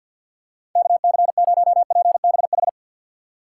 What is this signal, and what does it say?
Morse code, heard as a keyed tone, about 52 wpm, 700 Hz. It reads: KC0PBH
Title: USA Call Signs - 4x Character Spacing - 50wpm